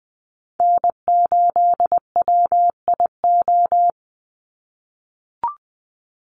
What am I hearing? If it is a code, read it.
Morse code, heard as a keyed tone, about 19 wpm, 700 Hz. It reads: N8WIO
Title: USA Call Signs 20wpm